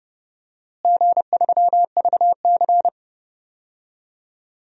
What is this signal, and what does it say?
Morse code, heard as a keyed tone, about 30 wpm, 700 Hz. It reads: G3VC
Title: DXCC-100 Call Signs - 4 Characters - Repeated 2x - 30wpm